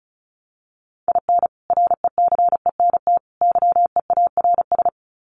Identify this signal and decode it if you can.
Morse code, heard as a keyed tone, about 34 wpm, 700 Hz. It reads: IN RECENT YEARS
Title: Sets of 3 Words 35wpm